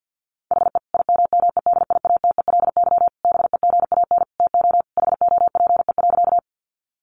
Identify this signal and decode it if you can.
Morse code, heard as a keyed tone, about 50 wpm, 700 Hz. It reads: HE IMMEDIATELY BEGAN TO HOPE.
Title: Sentences from Top 1100 Words 50wpm